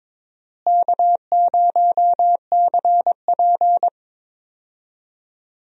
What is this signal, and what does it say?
Morse code, heard as a keyed tone, about 22 wpm, 700 Hz. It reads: K0CP
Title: USA Call Signs - 4 Characters - Repeated 2x - 22wpm